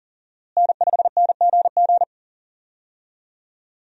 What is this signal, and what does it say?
Morse code, heard as a keyed tone, about 40 wpm, 700 Hz. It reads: N5NGG
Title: USA Call Signs - 5 Characters - Repeated 3x - 40wpm